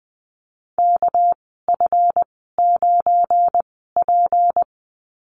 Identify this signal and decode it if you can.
Morse code, heard as a keyed tone, about 20 wpm, spaced as for 14 wpm, 700 Hz. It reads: KF9P
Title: 219 Lesson 35g - Number 0 - Call Signs 20wpm